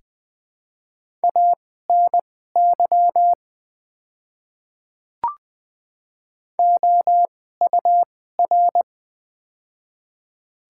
Morse code, heard as a keyed tone, about 20 wpm, spaced as for 14 wpm, 700 Hz. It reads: ANY OUR
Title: 2x Character Spacing - Top 200 Words - 20wpm